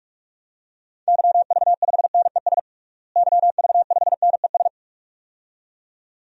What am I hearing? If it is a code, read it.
Morse code, heard as a keyed tone, about 41 wpm, 700 Hz. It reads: YV5NES YV5NES
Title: DXCC-100 Call Signs - 6 Characters - Repeated 2x - 45wpm